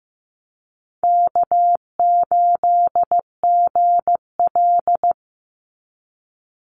Morse code, heard as a keyed tone, about 15 wpm, 700 Hz. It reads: K8GL
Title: Easy Call Signs - 15wpm